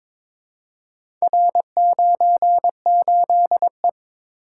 Morse code, heard as a keyed tone, about 23 wpm, 700 Hz. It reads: R98E
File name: practice / 22wpm / Four Letter-Number Combinations - 22wpm